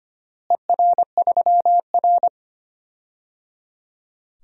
Morse code, heard as a keyed tone, about 25 wpm, 700 Hz. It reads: ER3R
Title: DXCC-100 Call Signs - 4 Characters - Repeated 3x - 25wpm